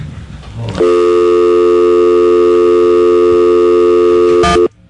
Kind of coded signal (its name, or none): telephone call progress tone